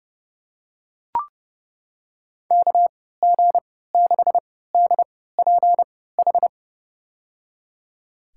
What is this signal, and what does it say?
Morse code, heard as a keyed tone, about 30 wpm, spaced as for 17 wpm, 700 Hz. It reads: KG6DPH
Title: USA Call Signs - 3x Character Spacing - 30wpm